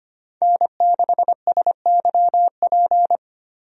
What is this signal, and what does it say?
Morse code, heard as a keyed tone, about 25 wpm, 700 Hz. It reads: N6SYP
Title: USA Call Signs - 5 Characters - 25wpm